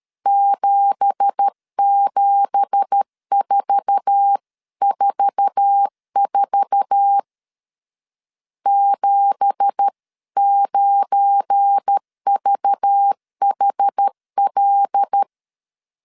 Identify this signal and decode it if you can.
Morse code, heard as a keyed tone, about 13 wpm, 800 Hz. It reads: 77444 79VHL
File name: CW / DET / 1 9 1500C S C 12wpm 3811